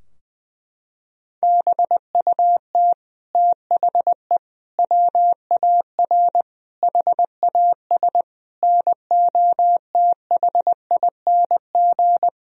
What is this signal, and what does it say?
Morse code, heard as a keyed tone, about 20 wpm, 700 Hz. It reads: BUT THE WAR HAS NOTHING
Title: Sentences from Top 1000 Words - Review Part 2 - 20wpm